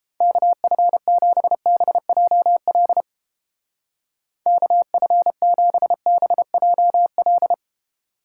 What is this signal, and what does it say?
Morse code, heard as a keyed tone, about 31 wpm, 700 Hz. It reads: KF7BJL KF7BJL